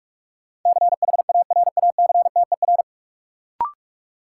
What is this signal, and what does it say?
Morse code, heard as a keyed tone, about 46 wpm, 700 Hz. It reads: CHARACTER
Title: Sentences from Top 1200 Words - New Word Drill - 45wpm